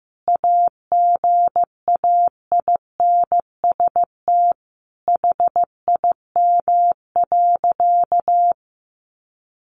Morse code, heard as a keyed tone, about 14 wpm, 700 Hz. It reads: AGAINST HIM.
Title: Sentences from Top 1300 Words 15wpm